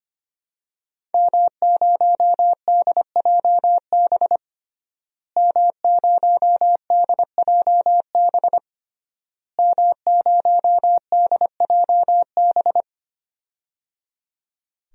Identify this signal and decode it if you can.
Morse code, heard as a keyed tone, about 25 wpm, 700 Hz. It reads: M0DJB M0DJB M0DJB
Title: DXCC-100 Call Signs - 5 Characters - Repeated 3x - 25wpm